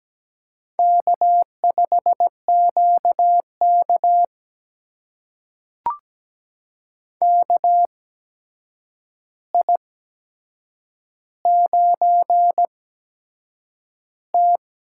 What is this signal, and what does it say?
Morse code, heard as a keyed tone, about 17 wpm, spaced as for 7 wpm, 700 Hz. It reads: K5QK KI9T
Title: USA Call Signs - 4 Characters - 8x Character Spacing - 17wpm